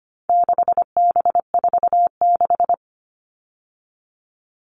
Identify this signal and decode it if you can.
Morse code, heard as a keyed tone, about 25 wpm, 700 Hz. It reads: 6B46
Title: All Cut Numbers - 4-Digit 25wpm